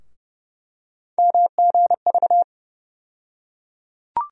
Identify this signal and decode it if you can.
Morse code, heard as a keyed tone, about 30 wpm, 700 Hz. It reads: MGV
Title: USA Call Sign Postfixes - 3 Letters 30wpm